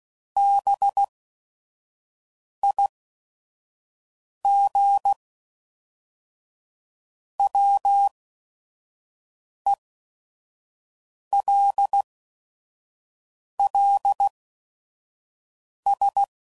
Morse code, heard as a keyed tone, about 16 wpm, spaced as for 5 wpm, 800 Hz. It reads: BIGWELLS
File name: CW Code Groups @5WPM / fromSM's / Cities